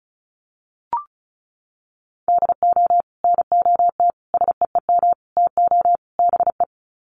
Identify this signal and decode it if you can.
Morse code, heard as a keyed tone, about 34 wpm, 700 Hz. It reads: DO NOT SEEM TO BE
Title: Sets of 5 Words Encore2 35wpm